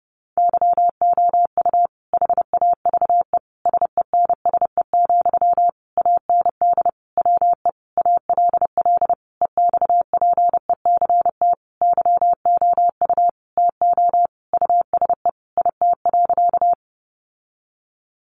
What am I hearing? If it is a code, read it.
Morse code, heard as a keyed tone, about 28 wpm, 700 Hz. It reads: YOU HAVE SENSE, AND WE ALL EXPECT YOU TO USE IT.
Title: Sentences from Top 1100 Words 30wpm